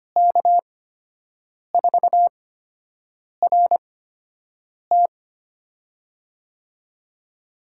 Morse code, heard as a keyed tone, about 25 wpm, spaced as for 7 wpm, 700 Hz. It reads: K4RT
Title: USA Call Signs - 5 Characters - 8x Character Spacing - 25wpm